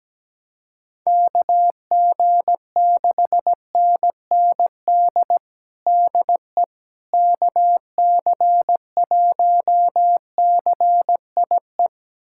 Morse code, heard as a keyed tone, about 17 wpm, 700 Hz. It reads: KG6NND DE KC1CIE
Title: USA Callsign DE Callsign - 6 Characters - 17wpm